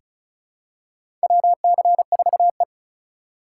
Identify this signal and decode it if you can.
Morse code, heard as a keyed tone, about 36 wpm, 700 Hz. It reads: WC4E
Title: USA Call Signs - 4 Characters - 4x Character Spacing - 35wpm